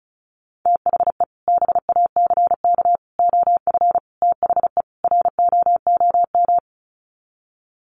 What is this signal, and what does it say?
Morse code, heard as a keyed tone, about 34 wpm, 700 Hz. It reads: THE BACK OF THE ROOM
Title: Sets of 5 Words Encore 35wpm